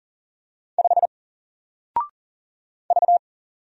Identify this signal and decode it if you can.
Morse code, heard as a keyed tone, about 40 wpm, 700 Hz. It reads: 5 V
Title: Single Character (Letter:Number:Punctuation:Prosign) 40wpm